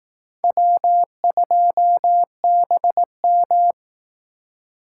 Morse code, heard as a keyed tone, about 18 wpm, 700 Hz. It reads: W2BM
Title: Speed Racing - USA 1x2 Call Signs 15wpm